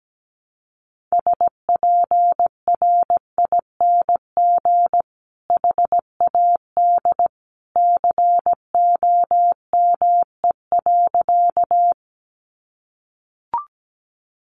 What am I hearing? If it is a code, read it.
Morse code, heard as a keyed tone, about 17 wpm, 700 Hz. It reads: SPRING HAD COME.
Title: Sentences from Top 1200 Words 17wpm